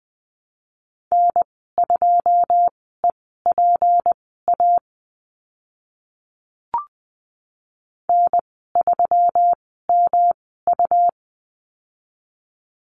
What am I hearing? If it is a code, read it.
Morse code, heard as a keyed tone, about 20 wpm, spaced as for 14 wpm, 700 Hz. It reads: N2EPA N3MU
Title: 186 Lesson 30g - Letter b - Call Signs 20wpm